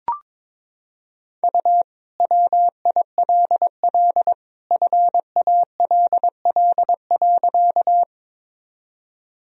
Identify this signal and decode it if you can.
Morse code, heard as a keyed tone, about 21 wpm, 700 Hz. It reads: U WILL FALL.
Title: Sentences from Top 500 Words and Abbreviations 22wpm